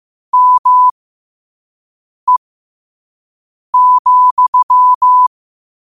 Morse code, heard as a keyed tone, about 15 wpm, spaced as for 5 wpm, 1000 Hz. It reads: ME,
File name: 5wpm